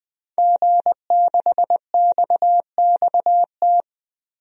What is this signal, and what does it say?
Morse code, heard as a keyed tone, about 20 wpm, 700 Hz. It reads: G6XXT
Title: DXCC-100 Call Signs - 5 Characters - 20wpm